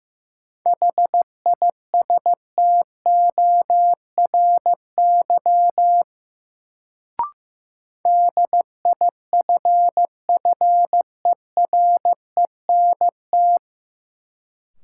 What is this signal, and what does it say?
Morse code, heard as a keyed tone, about 15 wpm, 700 Hz. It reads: HISTORY DIFFERENT
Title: Sentences from Top 600 Words - New Word Drill - 15wpm